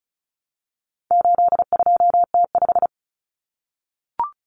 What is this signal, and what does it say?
Morse code, heard as a keyed tone, about 34 wpm, 700 Hz. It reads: 82T5
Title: Common Cut Numbers - 4-Digit 35wpm